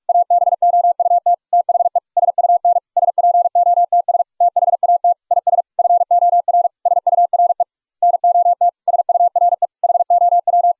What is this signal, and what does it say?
Morse code, heard as a keyed tone, about 42 wpm, 700 Hz. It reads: ABOUT THE SUN SPOTS THAT IS FOR SURE NOT SURE HOW